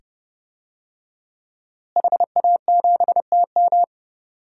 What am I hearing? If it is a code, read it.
Morse code, heard as a keyed tone, about 30 wpm, 700 Hz. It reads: HA7TM
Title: DXCC-100 Call Signs - 5 Characters - Repeated 2x - 30wpm